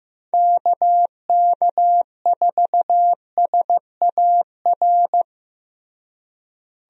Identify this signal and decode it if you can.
Morse code, heard as a keyed tone, about 15 wpm, 700 Hz. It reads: KK4SAR
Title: USA Call Signs - 6 Characters - Repeated 3x - 15wpm